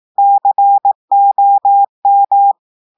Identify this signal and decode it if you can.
Morse code, heard as a keyed tone, about 18 wpm, 800 Hz. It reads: COM
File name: audios / portugais / portugais texte en clair 1 vitesse 18 mots minutes